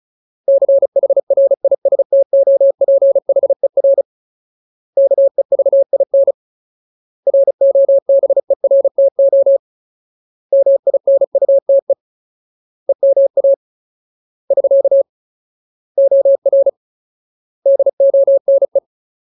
Morse code, heard as a keyed tone, about 34 wpm, 550 Hz. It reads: CHRISTOPHER KEVIN ROBERTO MINUTE EMA <SK> OR DONE
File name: Ultimate Word List / 35WPM Ultimate